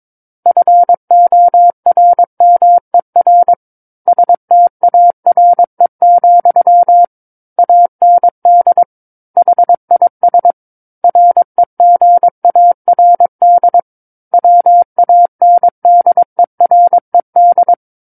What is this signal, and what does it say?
Morse code, heard as a keyed tone, about 22 wpm, 700 Hz. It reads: FORMER STARE, AND HIS REGARD WANDERED
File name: Book One / 13-Chapter 13 - 22 WPM